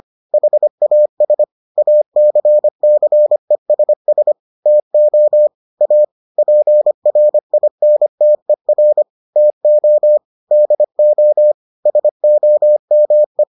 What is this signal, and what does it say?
Morse code, heard as a keyed tone, about 25 wpm, 600 Hz. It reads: HAS ACCESS TO A PRINTER TO DO SOME